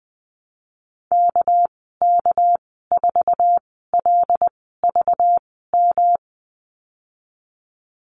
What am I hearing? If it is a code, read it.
Morse code, heard as a keyed tone, about 20 wpm, spaced as for 15 wpm, 700 Hz. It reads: KK4LVM
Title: 207 Lesson 33g - Letter j - Call Signs 20wpm